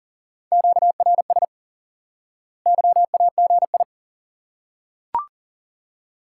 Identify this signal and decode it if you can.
Morse code, heard as a keyed tone, about 40 wpm, 700 Hz. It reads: QRS YAGI
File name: course / instant-qso-element-course-3 / IQE Course 3 - Lesson 89 - YAGI - Two Elements - 40wpm